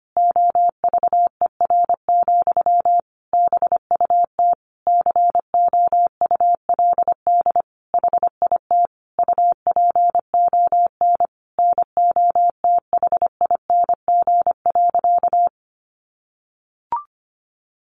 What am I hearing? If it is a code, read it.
Morse code, heard as a keyed tone, about 25 wpm, 700 Hz. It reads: OVER, BUT COULD HIT UPON NOTHING.
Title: Sentences from Top 1200 Words 25wpm